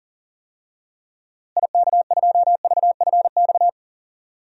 Morse code, heard as a keyed tone, about 39 wpm, 700 Hz. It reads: IK2VFX
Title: DXCC-100 Call Signs - 6 Characters - Repeated 2x - 40wpm